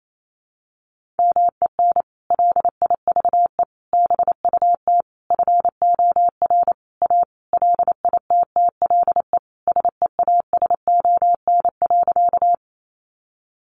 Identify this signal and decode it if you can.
Morse code, heard as a keyed tone, about 28 wpm, 700 Hz. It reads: MEN LIVE BUT FOR A LITTLE SEASON.
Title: Sentences from Top 1400 Words 28wpm